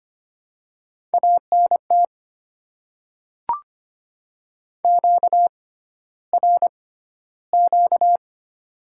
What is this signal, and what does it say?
Morse code, heard as a keyed tone, about 25 wpm, spaced as for 13 wpm, 690 Hz. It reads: ANT QRQ
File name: Top 90 QSO Elements - 6x Character Spacing - 25wpm